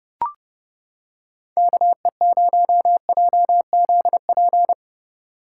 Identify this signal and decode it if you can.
Morse code, heard as a keyed tone, about 29 wpm, 700 Hz. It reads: KE0JZP